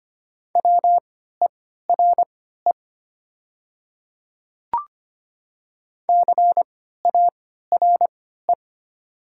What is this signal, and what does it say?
Morse code, heard as a keyed tone, about 25 wpm, spaced as for 14 wpm, 700 Hz. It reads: WERE CARE
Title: 3x Character Spacing - Top 300 Words - 25wpm